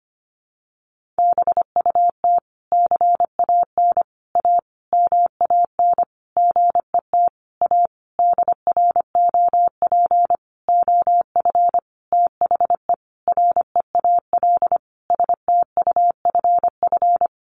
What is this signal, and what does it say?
Morse code, heard as a keyed tone, about 25 wpm, 700 Hz. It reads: BUT CAN A MAN GET A DROP OF THE REAL STUFF
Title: Sentences from Top 1400 Words 25wpm